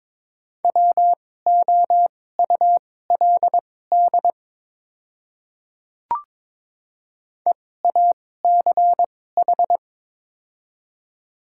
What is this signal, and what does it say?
Morse code, heard as a keyed tone, about 22 wpm, spaced as for 16 wpm, 700 Hz. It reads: WOULD EACH
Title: 2x Character Spacing - Top 100 Words - 22wpm